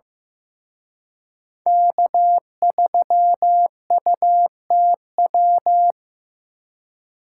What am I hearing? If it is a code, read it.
Morse code, heard as a keyed tone, about 18 wpm, 700 Hz. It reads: K3UTW
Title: Speed Racing - USA 1x3 Call Signs 15wpm